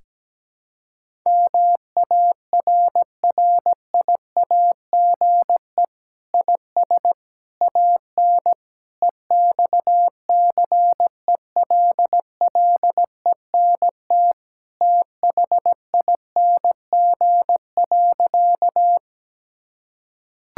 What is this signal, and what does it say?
Morse code, heard as a keyed tone, about 17 wpm, 700 Hz. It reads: MARRIAGE IS AN EXCELLENT THING.
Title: Sentences from Top 1000 Words - Review Part 5 - 17wpm